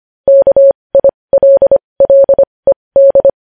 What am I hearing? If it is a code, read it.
Morse code, heard as a keyed tone, about 25 wpm, spaced as for 20 wpm, 550 Hz. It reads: KILLED
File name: podcast / News headlines-1x-25x20